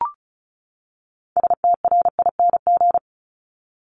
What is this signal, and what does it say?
Morse code, heard as a keyed tone, about 34 wpm, 700 Hz. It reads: STRING
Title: Top 1501-1750 Words 35wpm